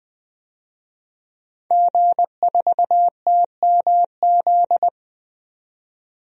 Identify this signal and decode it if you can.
Morse code, heard as a keyed tone, about 20 wpm, 700 Hz. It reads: G4TMZ
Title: DXCC-100 Call Signs - 5 Characters - 20wpm